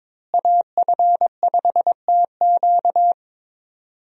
Morse code, heard as a keyed tone, about 22 wpm, 700 Hz. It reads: AF5TQ
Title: USA Call Signs - 5 Characters - Repeated 2x - 22wpm